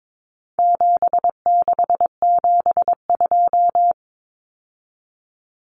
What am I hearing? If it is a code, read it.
Morse code, heard as a keyed tone, about 22 wpm, 700 Hz. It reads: 7672
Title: Common Cut Numbers - 4-Digit 22wpm